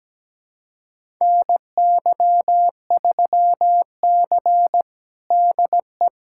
Morse code, heard as a keyed tone, about 17 wpm, 700 Hz. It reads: NY3C DE
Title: USA Callsign DE Callsign - 4 Characters - 17wpm